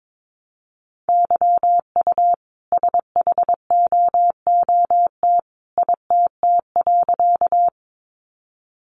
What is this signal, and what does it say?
Morse code, heard as a keyed tone, about 22 wpm, 700 Hz. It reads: YU SHOOT ITT.
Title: A Sound Of Thunder 22wpm